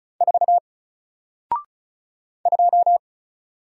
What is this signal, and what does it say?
Morse code, heard as a keyed tone, about 36 wpm, 700 Hz. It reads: 4 2
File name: Single-Letter-Number-35wpm